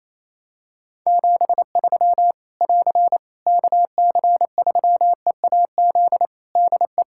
Morse code, heard as a keyed tone, about 28 wpm, 700 Hz. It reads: 73 <AR> KC3EAZ DE
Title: Practice QSO - Exchange 4 - Ending - 28wpm